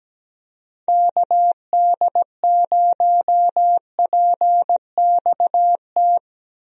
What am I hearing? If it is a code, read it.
Morse code, heard as a keyed tone, about 17 wpm, 700 Hz. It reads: KD0PXT
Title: USA Call Signs - 6 Characters - Repeated 2x - 17wpm